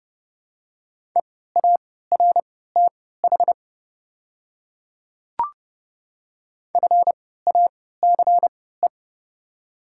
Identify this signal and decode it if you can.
Morse code, heard as a keyed tone, about 31 wpm, spaced as for 17 wpm, 700 Hz. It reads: EARTH FACE
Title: 3x Character Spacing - Top 300 Words - 30wpm